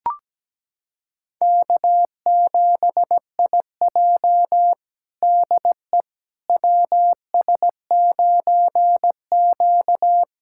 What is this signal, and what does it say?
Morse code, heard as a keyed tone, about 17 wpm, 700 Hz. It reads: K7IJ DE WS9Q
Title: USA Callsign DE Callsign - 4 Characters - 17wpm